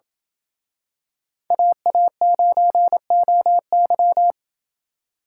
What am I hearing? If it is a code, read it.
Morse code, heard as a keyed tone, about 28 wpm, 700 Hz. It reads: AA9OY